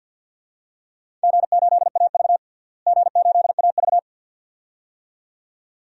Morse code, heard as a keyed tone, about 50 wpm, 700 Hz. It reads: G8AV G8AV
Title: DXCC-100 Call Signs - 4 Characters - Repeated 2x - 50wpm